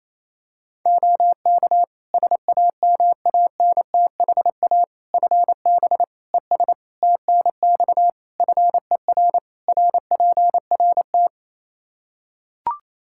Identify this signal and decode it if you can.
Morse code, heard as a keyed tone, about 28 wpm, 700 Hz. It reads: OK SAMANTHA FB ES TNX FER RPRT
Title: Practice QSO - Exchange 2 - Info - 28wpm